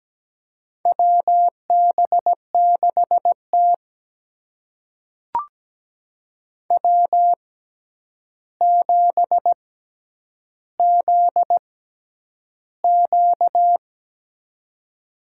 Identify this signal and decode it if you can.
Morse code, heard as a keyed tone, about 17 wpm, spaced as for 9 wpm, 700 Hz. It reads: WB6T W7ZQ
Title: USA Call Signs - 4 Characters - 6x Character Spacing - 17wpm